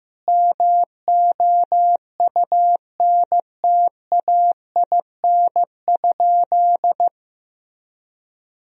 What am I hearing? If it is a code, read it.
Morse code, heard as a keyed tone, about 15 wpm, 700 Hz. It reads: MOUNTAIN?
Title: Sentences from Top 1000 Words - Review Part 1 - 15wpm